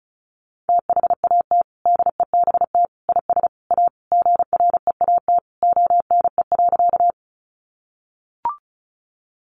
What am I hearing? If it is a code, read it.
Morse code, heard as a keyed tone, about 34 wpm, 700 Hz. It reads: THAT DEBT IS A GREAT ONE.
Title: Sentences from Top 1300 Words 35wpm